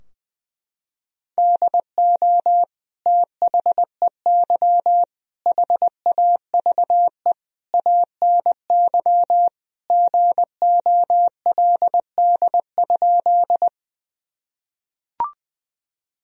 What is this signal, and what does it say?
Morse code, heard as a keyed tone, about 20 wpm, 700 Hz. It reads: DO THEY HAVE ANY GOLD?
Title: Sentences from Top 1000 Words - Review Part 3 - 20wpm